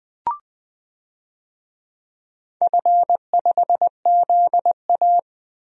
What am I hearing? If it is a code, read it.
Morse code, heard as a keyed tone, about 20 wpm, 700 Hz. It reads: F5ZA
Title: DXCC-100 Call Signs - 4 Characters - 20wpm